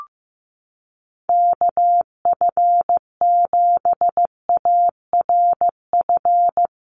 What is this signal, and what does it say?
Morse code, heard as a keyed tone, about 15 wpm, 700 Hz. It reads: KF7ARF
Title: USA Call Signs 15wpm